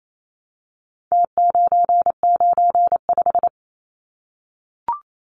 Morse code, heard as a keyed tone, about 28 wpm, 700 Hz. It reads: T995
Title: Numbers - Contest Style - 4-Digit 28wpm